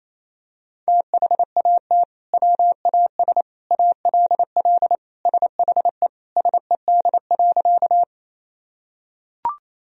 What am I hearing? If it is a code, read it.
Morse code, heard as a keyed tone, about 28 wpm, 700 Hz. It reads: THAT WAS ALL SHE SED.
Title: Sentences from Top 100 Words and Abbreviations 28wpm